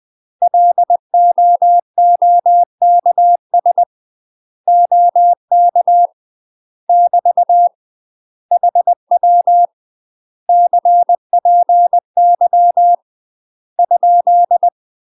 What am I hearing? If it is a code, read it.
Morse code, heard as a keyed tone, about 20 wpm, 700 Hz. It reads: LOOKS OK <BT> HW CPY ?
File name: UK MASTER QSOs TRAINING / 20WPM / UK MASTER QSOs 0070